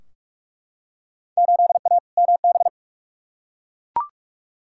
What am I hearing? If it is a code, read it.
Morse code, heard as a keyed tone, about 42 wpm, 700 Hz. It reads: 8A MB
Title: Field Day Exchanges Encore - 45wpm